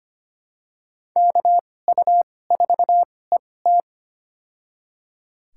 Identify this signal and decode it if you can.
Morse code, heard as a keyed tone, about 25 wpm, spaced as for 18 wpm, 700 Hz. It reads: KU4ET
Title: USA Call Signs - 2x Character Spacing - 25wpm